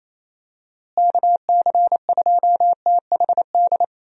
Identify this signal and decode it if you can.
Morse code, heard as a keyed tone, about 28 wpm, 700 Hz. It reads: KC2THD
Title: USA Call Signs - 6 Characters - Repeated 3x - 28wpm